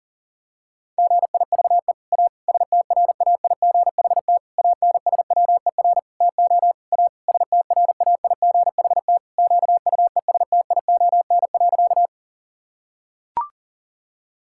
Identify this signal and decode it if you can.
Morse code, heard as a keyed tone, about 40 wpm, 690 Hz. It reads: GIVE A STRAIGHT ANSWER TO A STRAIGHT QUESTION.
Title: Sentences from Top 1000 Words - Review Part 6 - 40wpm